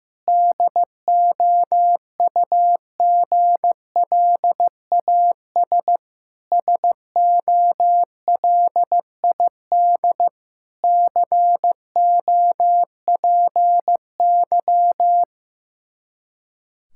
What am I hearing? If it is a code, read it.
Morse code, heard as a keyed tone, about 15 wpm, 700 Hz. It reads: DOUGLAS SOLID COPY
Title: Practice QSO - Exchange 3 - More Info - 15wpm